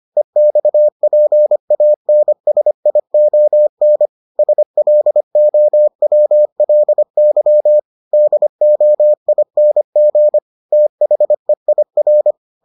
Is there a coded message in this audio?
Yes: Morse code from 0.2 to 12.3 s, heard as a keyed tone, about 25 wpm, 600 Hz. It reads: EXPANSION SLOWLY DOING THEIR